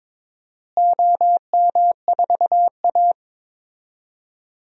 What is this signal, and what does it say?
Morse code, heard as a keyed tone, about 21 wpm, 700 Hz. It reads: OM4A
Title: DXCC-100 Call Signs - 4 Characters - Repeated 2x - 22wpm